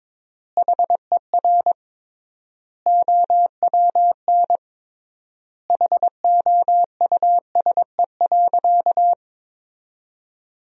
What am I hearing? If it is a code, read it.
Morse code, heard as a keyed tone, about 22 wpm, 700 Hz. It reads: HER OWN HOUSE.
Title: 3x Word-Spacing - Sentences from Top 300 Words - 22wpm